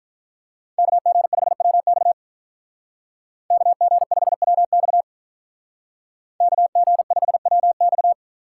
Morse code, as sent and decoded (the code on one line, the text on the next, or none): -.- --. ..... .-- -..- / -.- --. ..... .-- -..- / -.- --. ..... .-- -..-
KG5WX KG5WX KG5WX